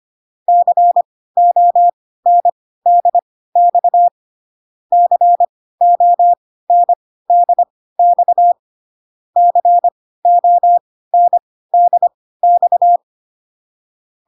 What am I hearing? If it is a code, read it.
Morse code, heard as a keyed tone, about 25 wpm, spaced as for 16 wpm, 700 Hz. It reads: CONDX CONDX CONDX